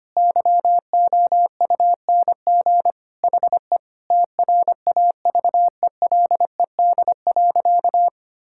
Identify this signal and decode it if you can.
Morse code, heard as a keyed tone, about 25 wpm, 700 Hz. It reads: YOUNG HE TRAVELED.
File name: practice / 25wpm / Sentences with 3-Word Phrases - 25wpm